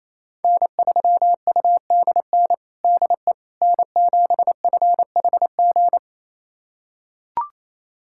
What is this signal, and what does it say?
Morse code, heard as a keyed tone, about 28 wpm, 700 Hz. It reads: N3UDN DE N7FHG
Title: USA Callsign DE Callsign - 5 Characters - 28wpm